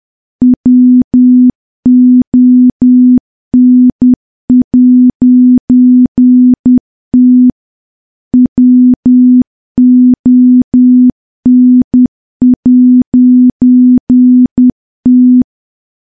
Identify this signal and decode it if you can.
Morse code, heard as a keyed tone, about 10 wpm, 250 Hz. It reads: WON'T WON'T
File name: morse / morse-training-10WPM